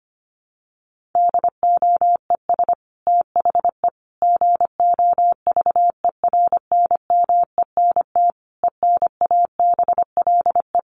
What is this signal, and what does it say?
Morse code, heard as a keyed tone, about 25 wpm, 700 Hz. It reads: DOES THE GOVERNMENT ENABLE
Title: Sentences from Top 1500 Words 25wpm